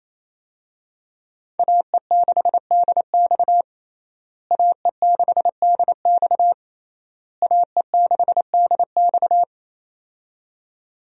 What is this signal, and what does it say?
Morse code, heard as a keyed tone, about 28 wpm, 700 Hz. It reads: AE6DX AE6DX AE6DX
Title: USA Call Signs - 5 Characters - Repeated 3x - 28wpm